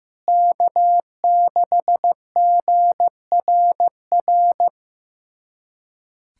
Morse code, heard as a keyed tone, about 15 wpm, 700 Hz. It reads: K6GRR
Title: USA Callsign DE Callsign - 5 Characters - 15wpm